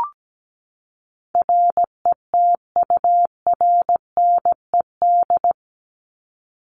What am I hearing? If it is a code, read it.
Morse code, heard as a keyed tone, about 17 wpm, 700 Hz. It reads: RETURNED
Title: ED Suffix - Top 100 Words with Base Words 17wpm